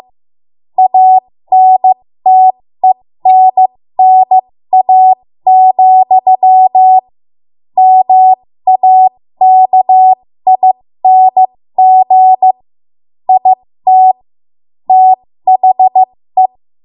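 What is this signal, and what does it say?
Morse code, heard as a keyed tone, about 15 wpm, spaced as for 13 wpm, 750 Hz. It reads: ANTENNA, MAKING IT THE
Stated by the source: 13 wpm Morse code practice archive transmission.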